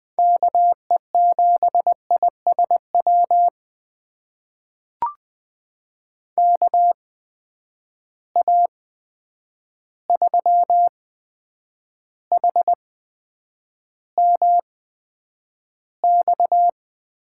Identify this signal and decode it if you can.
Morse code, heard as a keyed tone, about 20 wpm, spaced as for 9 wpm, 700 Hz. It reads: KE7ISW KA3HMX
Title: USA Call Signs - 6 Characters - 8x Character Spacing - 20wpm